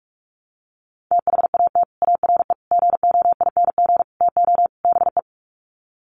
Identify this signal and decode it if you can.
Morse code, heard as a keyed tone, about 44 wpm, 700 Hz. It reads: THAT ARE GOING TO BE
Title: Sets of 5 Words Encore2 45wpm